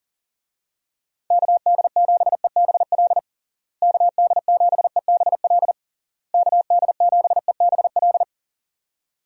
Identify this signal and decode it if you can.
Morse code, heard as a keyed tone, about 40 wpm, 700 Hz. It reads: KD7EBL KD7EBL KD7EBL
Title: USA Call Signs - 6 Characters - Repeated 3x - 40wpm